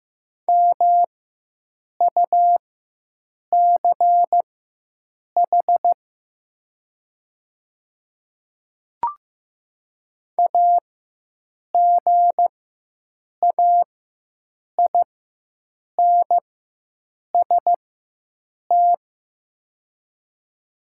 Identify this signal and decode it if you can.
Morse code, heard as a keyed tone, about 15 wpm, spaced as for 7 wpm, 690 Hz. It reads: MUCH AGAINST
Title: 4x Character Spacing - Top 500 Words - 15wpm